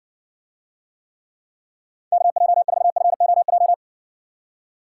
Morse code, heard as a keyed tone, about 60 wpm, 700 Hz. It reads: KJ4FOJ